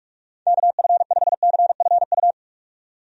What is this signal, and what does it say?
Morse code, heard as a keyed tone, about 46 wpm, 700 Hz. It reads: KF5CFU
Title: USA Call Signs - 6 Characters - 45wpm